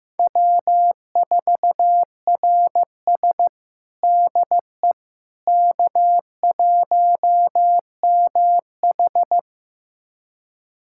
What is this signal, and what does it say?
Morse code, heard as a keyed tone, about 16 wpm, 700 Hz. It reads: W4RS DE K1MH
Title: USA Callsign DE Callsign - 4 Characters - 15wpm